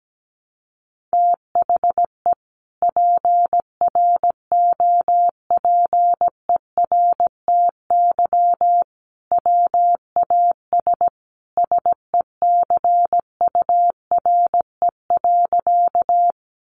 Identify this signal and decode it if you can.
Morse code, heard as a keyed tone, about 17 wpm, 700 Hz. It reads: THE PROPERTY WAS SECURE.
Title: Sentences from Top 1200 Words 17wpm